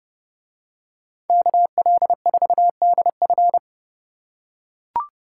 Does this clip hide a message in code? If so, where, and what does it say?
Yes: Morse code from 1.3 to 3.6 s, heard as a keyed tone, about 29 wpm, 700 Hz. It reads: KL4DF